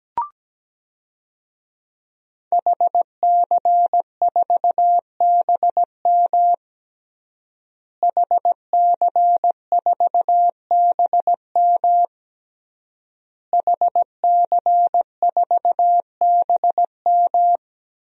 Morse code, heard as a keyed tone, about 17 wpm, 700 Hz. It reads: HC4BM HC4BM HC4BM
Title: DXCC-100 Call Signs - 5 Characters - Repeated 3x - 17wpm